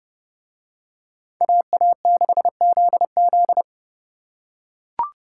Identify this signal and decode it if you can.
Morse code, heard as a keyed tone, about 31 wpm, 700 Hz. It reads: AA6ZZ